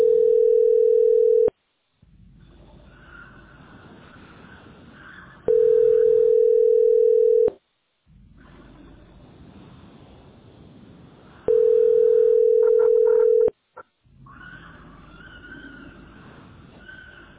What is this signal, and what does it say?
Ringback tone (North America), 440 + 480 Hz.